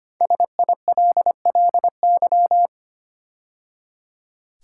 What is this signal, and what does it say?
Morse code, heard as a keyed tone, about 25 wpm, 700 Hz. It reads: SILLY
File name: 5 Letter Words - First 3 Letters Given - 25wpm